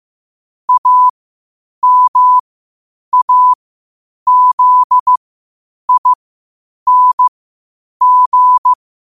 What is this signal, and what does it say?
Morse code, heard as a keyed tone, about 15 wpm, spaced as for 8 wpm, 1000 Hz. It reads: AMAZING
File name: qotdfiles / 8wpm